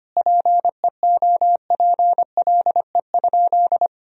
Morse code, heard as a keyed tone, about 25 wpm, 700 Hz. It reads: PEOPLE?
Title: Sentences from Top 1000 Words - Review Part 3 - 25wpm